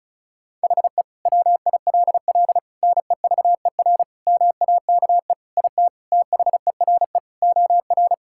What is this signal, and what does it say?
Morse code, heard as a keyed tone, about 34 wpm, 700 Hz. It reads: HE WILL NEVER MAKE IT THERE OR
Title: Sentences from Top 1600 Words 35wpm